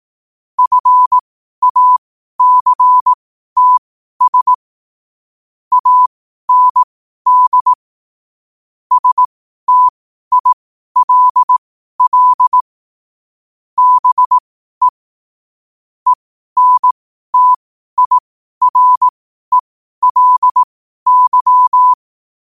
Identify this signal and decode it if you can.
Morse code, heard as a keyed tone, about 18 wpm, spaced as for 12 wpm, 1000 Hz. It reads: FACTS AND STILL BE ENTIRELY
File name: qotdfiles / 12wpm